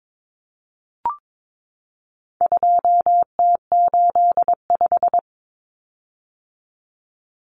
Morse code, heard as a keyed tone, about 22 wpm, 700 Hz. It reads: 2T85
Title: Common Cut Numbers - 4-Digit 22wpm